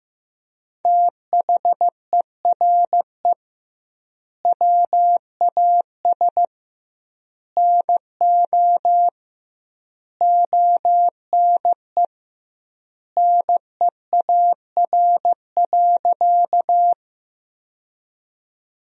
Morse code, heard as a keyed tone, about 15 wpm, 700 Hz. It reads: THERE WAS NO ONE NEAR.
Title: 2x Word-Spacing - Sentences from Top 200 Words - 15wpm